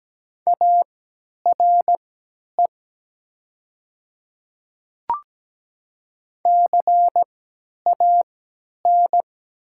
Morse code, heard as a keyed tone, about 17 wpm, spaced as for 10 wpm, 700 Hz. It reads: ARE CAN
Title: 3x Character Spacing - Top 500 Words - 17wpm